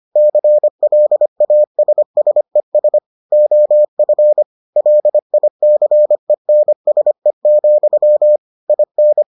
Morse code, heard as a keyed tone, about 25 wpm, 600 Hz. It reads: CLASSES OF LICENSE, IN